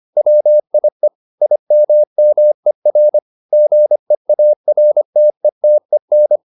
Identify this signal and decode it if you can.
Morse code, heard as a keyed tone, about 25 wpm, 600 Hz. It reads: WIE IMMER GEARTETEN